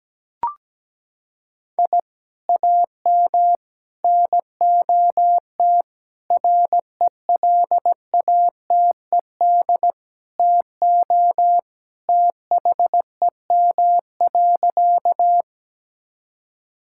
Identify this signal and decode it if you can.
Morse code, heard as a keyed tone, about 17 wpm, 700 Hz. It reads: I AM NOT RELATED TO THEM.
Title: Sentences from Top 1000 Words - Review Part 5 - 17wpm